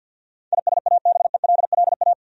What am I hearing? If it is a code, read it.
Morse code, heard as a keyed tone, about 52 wpm, 700 Hz. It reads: ISABELLA
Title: USA Common Names - 50wpm